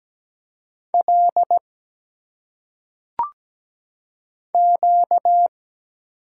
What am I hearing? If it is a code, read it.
Morse code, heard as a keyed tone, about 17 wpm, 700 Hz. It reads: L Q
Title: Single Letters - Phonetic Pronunciation 17wpm